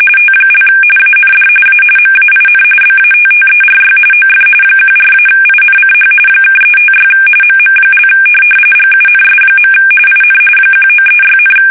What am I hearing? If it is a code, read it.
RTTY, about 45.45 baud. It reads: T) AND A MAIN BATTERY OF 14-INCH (356 MM) GUNS. IN 1936,
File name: fsk850 / USS-Massachusetts